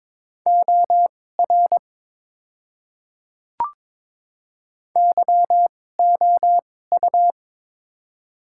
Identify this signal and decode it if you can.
Morse code, heard as a keyed tone, about 22 wpm, spaced as for 16 wpm, 700 Hz. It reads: OR YOU
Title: 2x Character Spacing - Top 100 Words - 22wpm